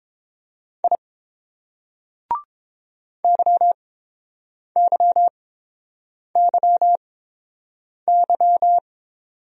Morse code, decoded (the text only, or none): I YYYY